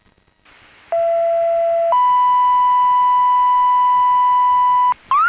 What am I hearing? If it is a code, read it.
Two-tone page, heard as a one-second tone, then a three-second tone, about 670 then 990 Hz. It is heard from 0.9 to 4.9 s.
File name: Indian Lake Fire 2026 07 28 19 02 52